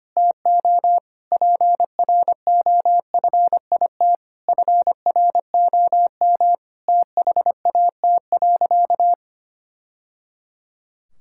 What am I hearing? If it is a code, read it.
Morse code, heard as a keyed tone, about 25 wpm, 700 Hz. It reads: TO PROFIT FROM THAT.
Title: Sentences from Top 1000 Words - Review Part 5 - 25wpm